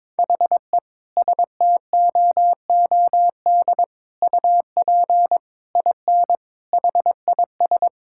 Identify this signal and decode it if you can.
Morse code, heard as a keyed tone, about 22 wpm, 700 Hz. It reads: HE STOOD UP IN HIS
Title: Sentences from Top 1600 Words 22wpm